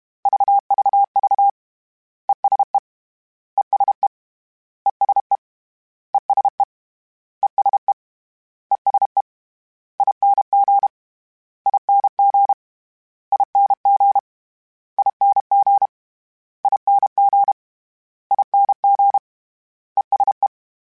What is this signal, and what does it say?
Morse code, heard as a keyed tone, about 32 wpm, 800 Hz. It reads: VVV ESE ESE ESE ESE ESE ESE ING ING ING ING ING ING ESE
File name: Short Runs / Short Runs-mp3s / ESE ING @30